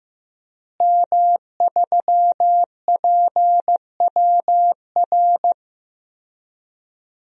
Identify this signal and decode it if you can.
Morse code, heard as a keyed tone, about 16 wpm, 700 Hz. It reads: M3PWR
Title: DXCC-100 Call Signs - 5 Characters - Repeated 2x - 15wpm